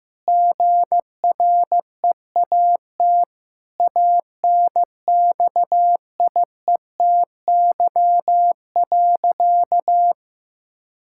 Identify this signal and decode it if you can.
Morse code, heard as a keyed tone, about 15 wpm, 700 Hz. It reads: GREAT ANXIETY.
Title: Sentences from Top 1600 Words 15wpm